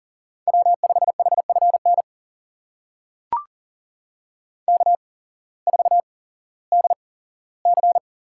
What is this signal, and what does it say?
Morse code, heard as a keyed tone, about 40 wpm, spaced as for 19 wpm, 700 Hz. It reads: W5HFN K4DC
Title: USA Call Signs - 8x Character Spacing - 40wpm